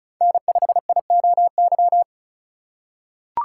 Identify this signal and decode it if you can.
Morse code, heard as a keyed tone, about 36 wpm, 700 Hz. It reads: N5IOY
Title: USA Call Signs - 5 Characters - 3x Character Spacing - 35wpm